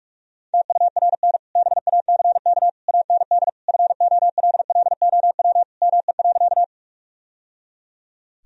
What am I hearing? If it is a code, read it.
Morse code, heard as a keyed tone, about 46 wpm, 700 Hz. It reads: TURN BACK AND FOLLOW ME.
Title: Sentences from Top 1000 Words - Review Part 1 - 45wpm